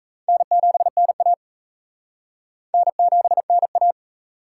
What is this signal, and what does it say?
Morse code, heard as a keyed tone, about 36 wpm, 700 Hz. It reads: N7NA N7NA